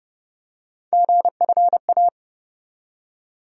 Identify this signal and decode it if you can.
Morse code, heard as a keyed tone, about 32 wpm, 700 Hz. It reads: GFA
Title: USA Call Sign Postfixes - 3 Letters 30wpm